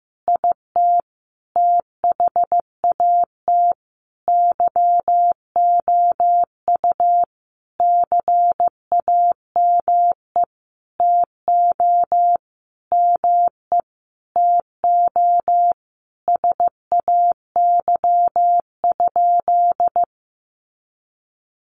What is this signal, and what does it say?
Morse code, heard as a keyed tone, about 14 wpm, 700 Hz. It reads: IT THAT YOU CAME TO ME TO SAY?
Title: Sentences from Top 200 Words 15wpm